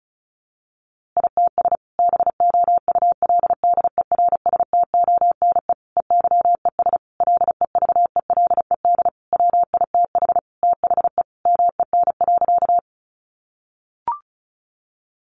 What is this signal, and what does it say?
Morse code, heard as a keyed tone, about 34 wpm, 700 Hz. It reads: ITS BOULDERSTONE EYES LEVELED WITH THE MEN.
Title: A Sound Of Thunder 35wpm